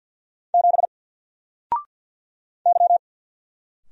Morse code, heard as a keyed tone, about 49 wpm, 700 Hz. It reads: 7 Y
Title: Single Letter-Number 50wpm